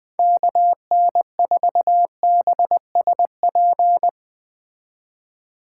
Morse code, heard as a keyed tone, about 20 wpm, 700 Hz. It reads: KN4BSP